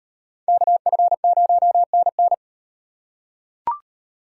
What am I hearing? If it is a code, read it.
Morse code, heard as a keyed tone, about 38 wpm, 700 Hz. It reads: KF0NN